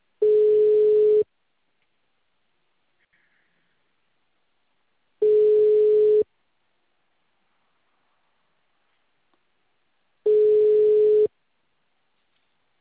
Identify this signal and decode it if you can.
Ringback tone (Europe, 425 Hz).
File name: out- -2044- - - .80426